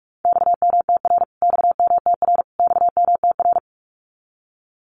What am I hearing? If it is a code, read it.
Morse code, heard as a keyed tone, about 44 wpm, 700 Hz. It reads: XMTR XMTR XMTR
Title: Abbreviations Repeated 3x - 45wpm